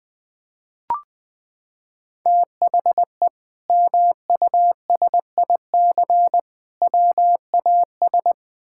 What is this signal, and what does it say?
Morse code, heard as a keyed tone, about 20 wpm, 700 Hz. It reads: THE MUSIC WAS
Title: Sentences from Top 1000 Words - Review Part 6 - 20wpm